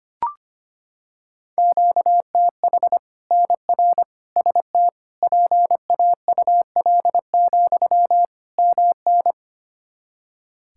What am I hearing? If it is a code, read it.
Morse code, heard as a keyed tone, about 25 wpm, 700 Hz. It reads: QTH NR ST PAUL, MN
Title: K1USN SST QSO Practice 25wpm